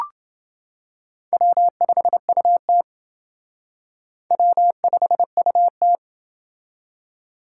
Morse code, heard as a keyed tone, about 28 wpm, 700 Hz. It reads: W5UT W5UT